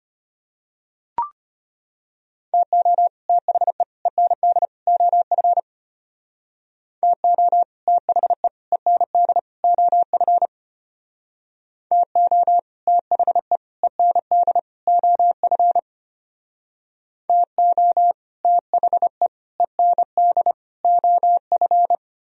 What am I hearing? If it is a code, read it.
Morse code, heard as a keyed tone, about 33 wpm, 700 Hz. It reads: TO THE END OF TO THE END OF TO THE END OF TO THE END OF